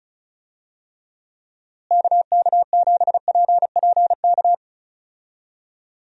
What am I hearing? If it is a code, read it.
Morse code, heard as a keyed tone, about 34 wpm, 700 Hz. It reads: KK7PPK
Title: USA Call Signs - 6 Characters - Repeated 2x - 35wpm